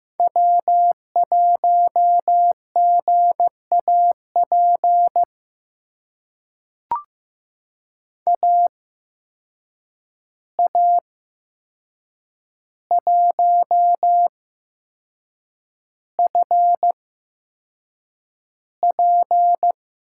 Morse code, heard as a keyed tone, about 15 wpm, spaced as for 6 wpm, 700 Hz. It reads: W1GAP AA1FP
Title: USA Call Signs - 5 Characters - 8x Character Spacing - 15wpm